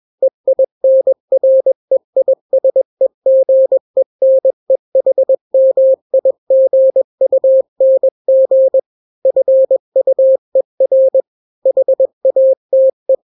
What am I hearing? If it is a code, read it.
Morse code, heard as a keyed tone, about 21 wpm, 530 Hz. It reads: EINREISEGENEHMIGUNG FUER HATE
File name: daily / 20wpm